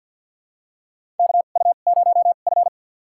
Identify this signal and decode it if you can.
Morse code, heard as a keyed tone, about 53 wpm, spaced as for 37 wpm, 700 Hz. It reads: KU0F
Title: USA Call Signs - 4 Characters - 2x Character Spacing - 50wpm